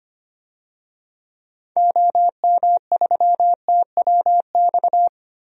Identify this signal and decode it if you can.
Morse code, heard as a keyed tone, about 25 wpm, 700 Hz. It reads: OM3TWX
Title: DXCC-100 Call Signs - 6 Characters - 25wpm